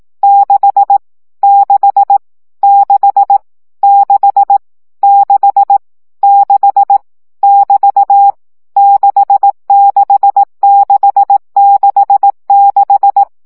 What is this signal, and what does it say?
Morse code, heard as a keyed tone, about 18 wpm, 800 Hz. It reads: 6 6 6 6 6 6 <BT> 66666
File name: CW gyakorlas 6 15 baud